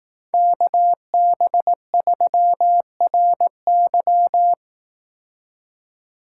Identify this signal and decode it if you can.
Morse code, heard as a keyed tone, about 18 wpm, 700 Hz. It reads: KB3RY
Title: Speed Racing - USA Call Signs 15wpm